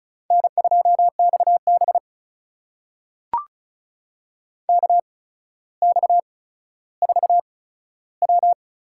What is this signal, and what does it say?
Morse code, heard as a keyed tone, about 36 wpm, spaced as for 15 wpm, 700 Hz. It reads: N2XB KX4W
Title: USA Call Signs - 4 Characters - 8x Character Spacing - 35wpm